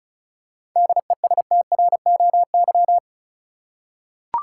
Morse code, heard as a keyed tone, about 36 wpm, 700 Hz. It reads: DESTROY
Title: Seven Letter Words - 35wpm